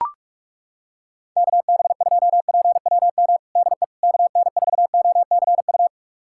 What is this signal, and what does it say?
Morse code, heard as a keyed tone, about 42 wpm, 700 Hz. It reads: KB2PWM DE KN4OCU
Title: USA Callsign DE Callsign - 6 Characters - 45wpm